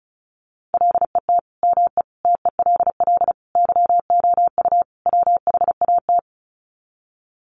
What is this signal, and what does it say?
Morse code, heard as a keyed tone, about 34 wpm, 700 Hz. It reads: LET ME TELL YOU WHAT
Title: Sets of 5 Words Encore 35wpm